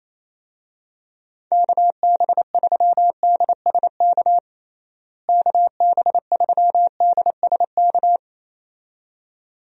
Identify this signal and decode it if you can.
Morse code, heard as a keyed tone, about 28 wpm, 700 Hz. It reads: KB3DSK KB3DSK
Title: USA Call Signs - 6 Characters - Repeated 2x - 28wpm